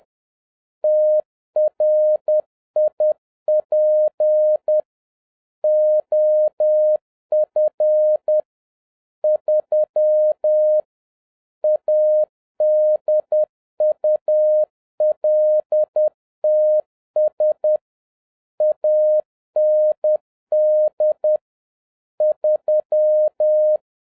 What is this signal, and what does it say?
Morse code, heard as a keyed tone, about 10 wpm, 610 Hz. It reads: TRIP OF 3 ADULTS AND 3